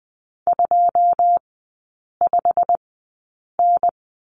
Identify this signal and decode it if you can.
Morse code, heard as a keyed tone, about 20 wpm, spaced as for 8 wpm, 700 Hz. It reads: 25N
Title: 138 Lesson 23e - Letter f - Three Characters 20wpm